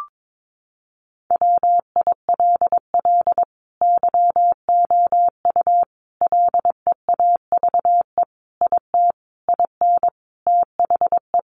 Morse code, heard as a keyed tone, about 22 wpm, 700 Hz. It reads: WILL YOU LEAVE IT IN THE
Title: Sentences from Top 1300 Words 22wpm